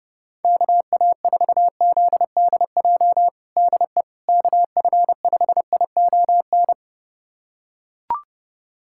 Morse code, heard as a keyed tone, about 32 wpm, 700 Hz. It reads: KA4ZDJ DE KF5ION
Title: USA Callsign DE Callsign - 6 Characters - 30wpm